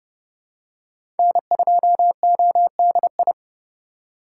Morse code, heard as a keyed tone, about 30 wpm, 700 Hz. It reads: N2ODI
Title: USA Call Signs - 8x Character Spacing - 30wpm